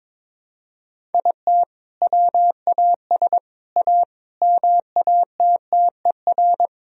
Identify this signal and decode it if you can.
Morse code, heard as a keyed tone, about 21 wpm, 700 Hz. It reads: IT WAS A MATTER
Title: Sentences from Top 1000 Words - Review Part 4 - 22wpm